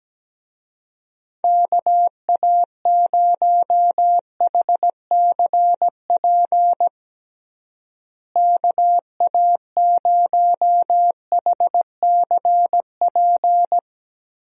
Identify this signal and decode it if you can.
Morse code, heard as a keyed tone, about 17 wpm, 700 Hz. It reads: KA0HCP KA0HCP
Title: USA Call Signs - 6 Characters - Repeated 3x - 17wpm